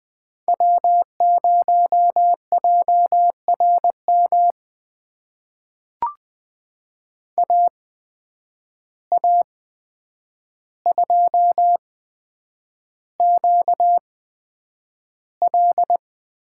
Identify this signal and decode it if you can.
Morse code, heard as a keyed tone, about 20 wpm, spaced as for 9 wpm, 700 Hz. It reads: W0JRM AA2QL
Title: USA Call Signs - 5 Characters - 8x Character Spacing - 20wpm